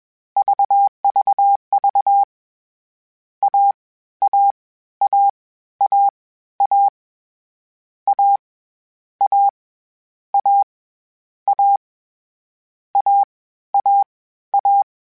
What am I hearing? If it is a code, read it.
Morse code, heard as a keyed tone, about 21 wpm, 800 Hz. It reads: VVV A A A A A A A A A A A A
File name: Difficult Combos / Difficult Combos-mp3s / AWJ @20